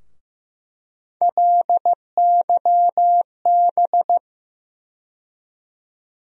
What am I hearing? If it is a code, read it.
Morse code, heard as a keyed tone, about 15 wpm, 700 Hz. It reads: LYB
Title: USA Call Sign Postfixes - 3 Letters 15wpm